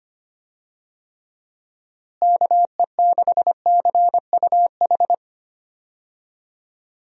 Morse code, heard as a keyed tone, about 25 wpm, 700 Hz. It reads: KE6CUH
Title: USA Call Signs - 6 Characters - Repeated 3x - 25wpm